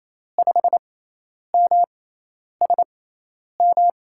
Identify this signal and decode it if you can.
Morse code, heard as a keyed tone, about 28 wpm, spaced as for 10 wpm, 700 Hz. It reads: 5MSM
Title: USA Call Signs - 6 Characters - 6x Character Spacing - 28wpm